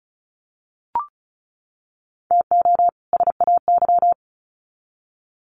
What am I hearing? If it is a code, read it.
Morse code, heard as a keyed tone, about 34 wpm, 700 Hz. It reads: TO SAY
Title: Sets of 2 Words 35wpm